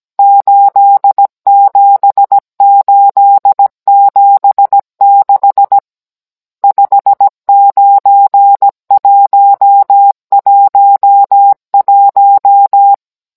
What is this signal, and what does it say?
Morse code, heard as a keyed tone, about 17 wpm, 800 Hz. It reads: 87876 59111
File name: Cours-NR18-16mm